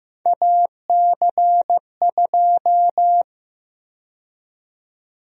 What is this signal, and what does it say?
Morse code, heard as a keyed tone, about 15 wpm, 700 Hz. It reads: AC2
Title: USA Call Sign Prefixes - Repeated 2x - 15wpm